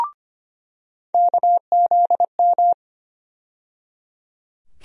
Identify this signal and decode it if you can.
Morse code, heard as a keyed tone, about 25 wpm, 700 Hz. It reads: KZM
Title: USA Call Sign Postfixes - 3 Letters 25wpm